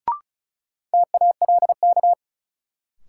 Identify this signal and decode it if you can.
Morse code, heard as a keyed tone, about 34 wpm, 700 Hz. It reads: TALK
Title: Sentences from Top 400 Words - New Word Drill - 35wpm